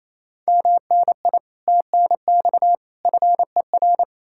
Morse code, heard as a keyed tone, about 37 wpm, 700 Hz. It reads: MNI TNX FER